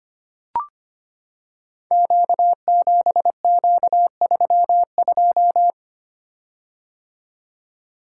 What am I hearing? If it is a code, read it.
Morse code, heard as a keyed tone, about 25 wpm, 700 Hz. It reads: Q7Q32
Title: Code Groups - 5 Letter:Numbers - 25wpm